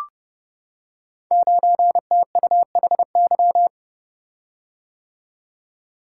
Morse code, heard as a keyed tone, about 30 wpm, 700 Hz. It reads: 9TUHY
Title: Code Groups - 5 Letter:Numbers - 30wpm